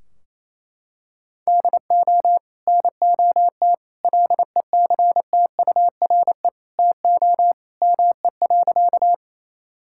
Morse code, heard as a keyed tone, about 28 wpm, 700 Hz. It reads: DO NOT LECTURE TO ME.
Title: Sentences from Top 1600 Words 28wpm